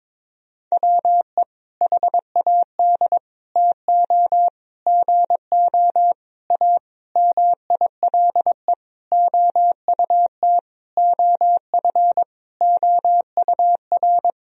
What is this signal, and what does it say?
Morse code, heard as a keyed tone, about 22 wpm, 700 Hz. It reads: WE HAD TO GO A MILE OUT OF OUR
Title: Sentences from Top 1000 Words - Review Part 1 - 22wpm